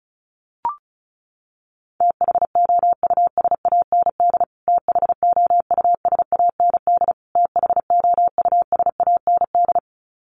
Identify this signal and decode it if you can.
Morse code, heard as a keyed tone, about 34 wpm, 700 Hz. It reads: THOUSAND THOUSAND THOUSAND
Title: Top 500 Words Repeated 35wpm